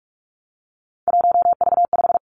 Morse code, heard as a keyed tone, about 46 wpm, 700 Hz. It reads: 1V5
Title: Three Challenging Letter-Number Combinations 45wpm